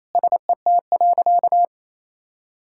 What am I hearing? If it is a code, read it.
Morse code, heard as a keyed tone, about 28 wpm, 690 Hz. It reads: SET.
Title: Sentences from Top 1000 Words - Review Part 6 - 28wpm